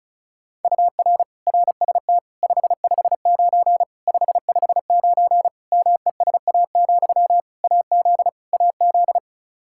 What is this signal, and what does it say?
Morse code, heard as a keyed tone, about 36 wpm, 700 Hz. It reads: UR RST 559 559 MESA, AZ AZ
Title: Practice QSO - Exchange 1 - Beginning - 35wpm